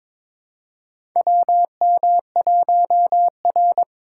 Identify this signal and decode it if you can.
Morse code, heard as a keyed tone, about 21 wpm, 700 Hz. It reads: WM1R
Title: USA Call Signs - 4 Characters - 2x Character Spacing - 22wpm